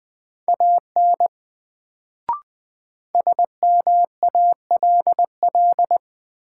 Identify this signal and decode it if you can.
Morse code, heard as a keyed tone, about 20 wpm, 700 Hz. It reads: AN SMALL
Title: Sets of 2 Words - New Word Drill - 20wpm